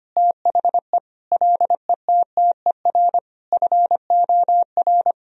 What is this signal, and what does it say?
Morse code, heard as a keyed tone, about 25 wpm, 700 Hz. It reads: THE LETTER FOR
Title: Sentences from Top 1000 Words - Review Part 2 - 25wpm